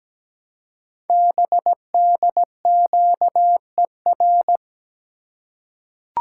Morse code, heard as a keyed tone, about 17 wpm, 700 Hz. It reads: BDQER
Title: Code Groups - 5 Letters - 17wpm